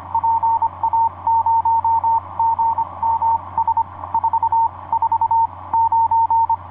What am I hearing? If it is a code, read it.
Morse code, heard as a keyed tone, about 25 wpm, 910 Hz, with radio noise. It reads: PA0GMS 4 4 9
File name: oa4-dl1nl pa0gms etc 17m cw 04feb2010 1546